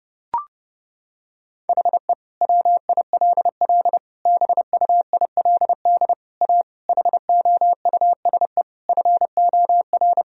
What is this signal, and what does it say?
Morse code, heard as a keyed tone, about 32 wpm, 700 Hz. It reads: HE WILL BUILD A HOUSE FOR
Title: Sentences from Top 1000 Words - Review Part 1 - 30wpm